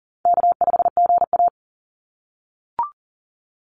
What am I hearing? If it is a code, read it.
Morse code, heard as a keyed tone, about 40 wpm, 700 Hz. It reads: K5GA
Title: Easy Call Signs - 40wpm